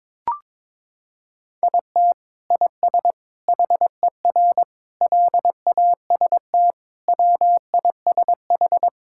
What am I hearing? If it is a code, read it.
Morse code, heard as a keyed tone, about 22 wpm, 700 Hz. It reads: IT IS HER LAST WISH
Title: Sentences from Top 1600 Words 22wpm